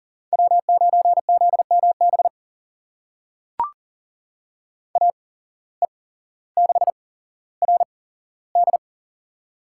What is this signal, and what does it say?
Morse code, heard as a keyed tone, about 39 wpm, spaced as for 17 wpm, 700 Hz. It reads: W9ZMB AE6RD
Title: USA Call Signs - 5 Characters - 8x Character Spacing - 40wpm